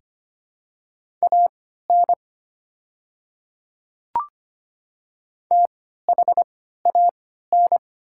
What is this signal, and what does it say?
Morse code, heard as a keyed tone, about 25 wpm, spaced as for 14 wpm, 690 Hz. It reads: AN THAN
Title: 3x Character Spacing - Top 200 Words - 25wpm